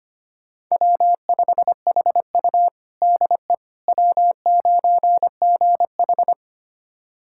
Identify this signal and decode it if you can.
Morse code, heard as a keyed tone, about 25 wpm, 700 Hz. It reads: W5HU DE W9GH
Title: USA Callsign DE Callsign - 4 Characters - 25wpm